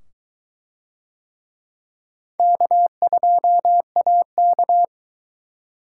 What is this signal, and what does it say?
Morse code, heard as a keyed tone, about 22 wpm, 700 Hz. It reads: K2AK